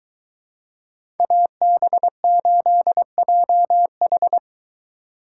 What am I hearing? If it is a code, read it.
Morse code, heard as a keyed tone, about 22 wpm, 700 Hz. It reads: AB8JH